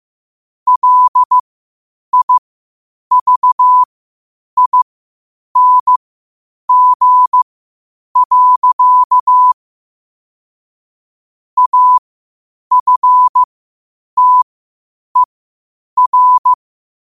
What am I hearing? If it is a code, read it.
Morse code, heard as a keyed tone, about 15 wpm, spaced as for 8 wpm, 1000 Hz. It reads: LIVING. AFTER